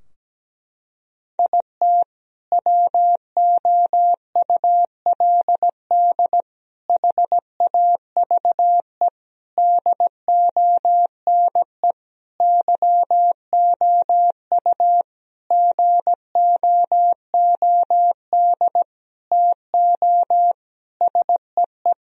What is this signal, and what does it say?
Morse code, heard as a keyed tone, about 17 wpm, 700 Hz. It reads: IT WOULD HAVE DONE YOU GOOD TO SEE
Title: Sentences from Top 1000 Words - Review Part 2 - 17wpm